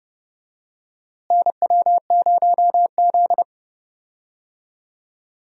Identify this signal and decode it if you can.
Morse code, heard as a keyed tone, about 32 wpm, 700 Hz. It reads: NW0Z
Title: USA Call Signs - 4 Characters - Repeated 3x - 30wpm